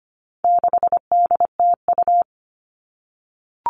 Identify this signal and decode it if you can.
Morse code, heard as a keyed tone, about 25 wpm, 700 Hz. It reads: 6DTU
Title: All Cut Numbers - 4-Digit 25wpm